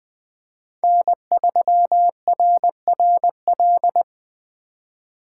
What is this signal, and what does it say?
Morse code, heard as a keyed tone, about 20 wpm, 700 Hz. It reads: N3RRL
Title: Speed Racing - USA 1x3 Call Signs 17wpm